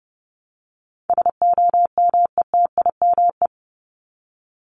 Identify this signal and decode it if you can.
Morse code, heard as a keyed tone, about 30 wpm, 700 Hz. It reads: SOMETIME
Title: ES:S Suffix - Top 100 Words with Base Words 30wpm